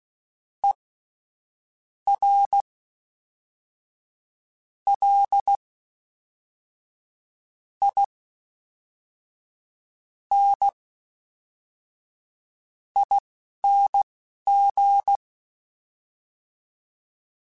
Morse code, heard as a keyed tone, about 16 wpm, spaced as for 5 wpm, 800 Hz. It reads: ERLINING
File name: Special INGs / Special INGs-mp3s / Fast ing @10